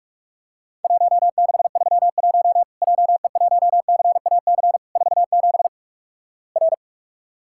Morse code, heard as a keyed tone, about 46 wpm, 700 Hz. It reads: 1631 JE1CAC 47 R
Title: Worked All Europe DX Contest - QTC Exchange - 45wpm